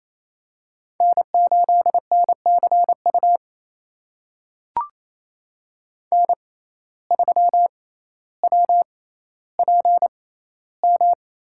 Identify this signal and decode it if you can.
Morse code, heard as a keyed tone, about 28 wpm, spaced as for 14 wpm, 700 Hz. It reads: N8NCU N3WPM
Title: USA Call Signs - 5 Characters - 6x Character Spacing - 28wpm